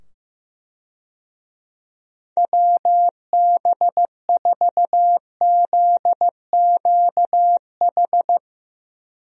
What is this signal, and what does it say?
Morse code, heard as a keyed tone, about 15 wpm, 700 Hz. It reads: WB4ZQH
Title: USA Call Signs - 6 Characters - 15wpm